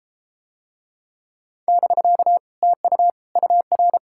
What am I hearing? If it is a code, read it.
Morse code, heard as a keyed tone, about 33 wpm, 700 Hz. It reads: <BK> TU UR